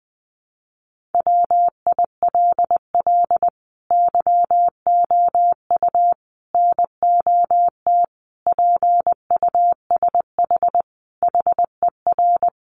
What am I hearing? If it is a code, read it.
Morse code, heard as a keyed tone, about 20 wpm, 700 Hz. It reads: WILL YOU NOT PUSH HER
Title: Sentences from Top 1300 Words 20wpm